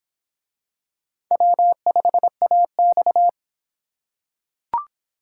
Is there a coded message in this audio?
Yes: Morse code from 1.3 to 3.3 s, heard as a keyed tone, about 25 wpm, 700 Hz. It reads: W5AX